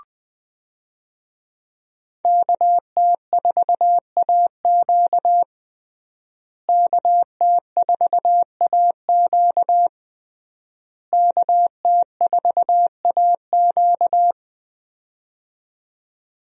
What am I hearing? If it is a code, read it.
Morse code, heard as a keyed tone, about 20 wpm, 700 Hz. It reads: KT4AQ KT4AQ KT4AQ
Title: USA Call Signs - 5 Characters - Repeated 3x - 20wpm